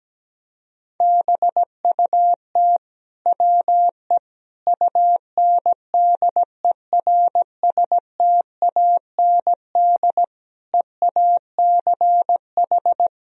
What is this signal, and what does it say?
Morse code, heard as a keyed tone, about 17 wpm, 700 Hz. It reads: BUT WE UNDERSTAND EACH
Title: Sentences from Top 1000 Words - Review Part 1 - 17wpm